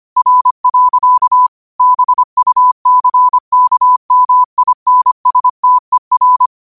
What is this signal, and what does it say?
Morse code, heard as a keyed tone, about 25 wpm, 1000 Hz. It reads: R. BUCKMINSTER